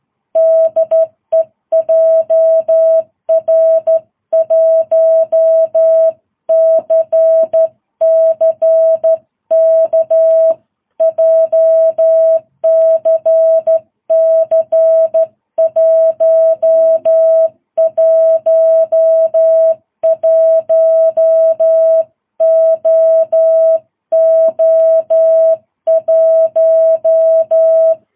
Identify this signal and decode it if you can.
Morse code, heard as a keyed tone, about 12 wpm, 650 Hz. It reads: DE JR1CCK JCC 111OO1
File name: jh8cbh / 2022cw